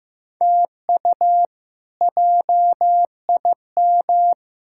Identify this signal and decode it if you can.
Morse code, heard as a keyed tone, about 15 wpm, 700 Hz. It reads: TU JIM
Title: K1USN SST QSO Practice - 15wpm